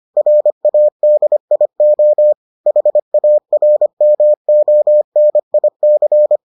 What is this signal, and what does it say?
Morse code, heard as a keyed tone, about 25 wpm, 600 Hz. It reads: RADIO HARMONIC